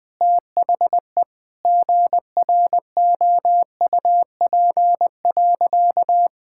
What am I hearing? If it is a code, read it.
Morse code, heard as a keyed tone, about 21 wpm, 700 Hz. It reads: THE GROUP.
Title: Sentences with 5-Word Phrases - 20wpm